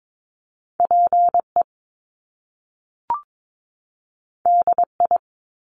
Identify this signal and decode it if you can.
Morse code, heard as a keyed tone, about 22 wpm, 700 Hz. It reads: PE DI
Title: Top Two Letter Combinations in Words 22wpm